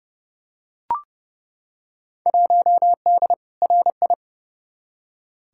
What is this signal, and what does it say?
Morse code, heard as a keyed tone, about 32 wpm, 700 Hz. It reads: 1D RI
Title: Field Day Exchanges - 30wpm